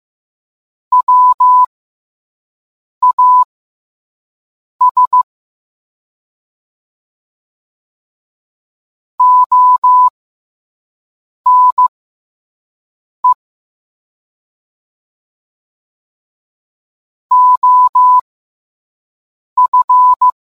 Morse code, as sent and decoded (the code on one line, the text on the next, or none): .-- .- ... / --- -. / --- ..-.
WAS ONE OF